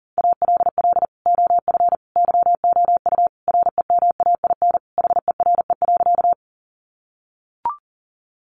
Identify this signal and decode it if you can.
Morse code, heard as a keyed tone, about 40 wpm, 700 Hz. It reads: ALL OF YOU REMAIN HERE.
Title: Sentences from Top 1200 Words 40wpm